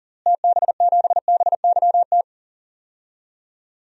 Morse code, heard as a keyed tone, about 40 wpm, 700 Hz. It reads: TB7BYT
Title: DXCC-100 Call Signs - 6 Characters - Repeated 3x - 40wpm